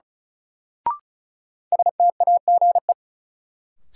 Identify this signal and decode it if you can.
Morse code, heard as a keyed tone, about 36 wpm, 700 Hz. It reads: STAGE
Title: Sentences from Top 1200 Words - New Word Drill Alt - 35wpm